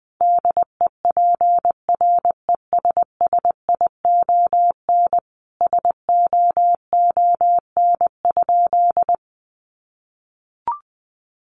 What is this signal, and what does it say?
Morse code, heard as a keyed tone, about 20 wpm, 700 Hz. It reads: DEPRESSION SOON?
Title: Sentences from Top 1300 Words 20wpm